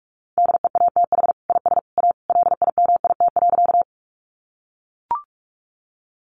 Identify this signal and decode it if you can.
Morse code, heard as a keyed tone, about 44 wpm, 700 Hz. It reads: DEATH IS A LIMIT.
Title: Sentences from Top 1500 Words 45wpm